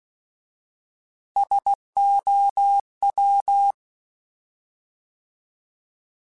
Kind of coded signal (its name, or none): Morse code